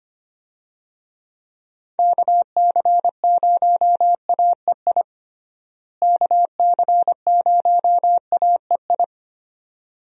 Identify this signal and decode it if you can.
Morse code, heard as a keyed tone, about 25 wpm, 700 Hz. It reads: KC0AEI KC0AEI
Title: USA Call Signs - 6 Characters - Repeated 2x - 25wpm